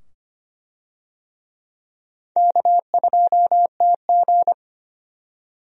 Morse code, heard as a keyed tone, about 25 wpm, 700 Hz. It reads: K2TG
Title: USA Call Signs - 4 Characters - 25wpm